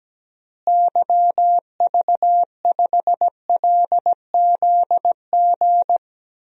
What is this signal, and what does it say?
Morse code, heard as a keyed tone, about 17 wpm, 700 Hz. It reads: YV5LZG
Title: DXCC-100 Call Signs - 6 Characters - Repeated 3x - 17wpm